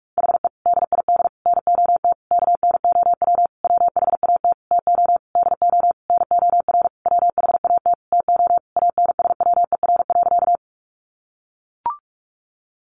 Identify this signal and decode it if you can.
Morse code, heard as a keyed tone, about 44 wpm, 700 Hz. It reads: HE DID NOT KNOW WHAT TO DO NOR WHAT TO ANSWER.
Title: Sentences from Top 1500 Words 45wpm